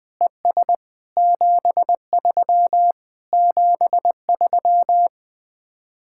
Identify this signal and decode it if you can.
Morse code, heard as a keyed tone, about 20 wpm, 700 Hz. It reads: ES 73 73